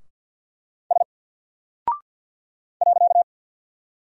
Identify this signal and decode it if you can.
Morse code, heard as a keyed tone, about 48 wpm, 700 Hz. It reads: S .
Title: Single Character (Letter:Number:Punctuation:Prosign) Alt 50wpm